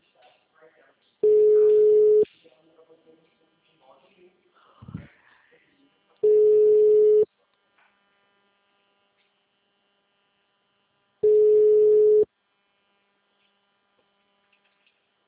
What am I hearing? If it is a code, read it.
Ringback tone (Europe, 425 Hz), with speech.